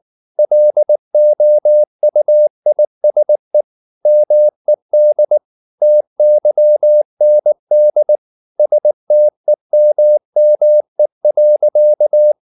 Morse code, heard as a keyed tone, about 19 wpm, 600 Hz. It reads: LOUISE MED TYND STEMME.
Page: Ferie 19 wpm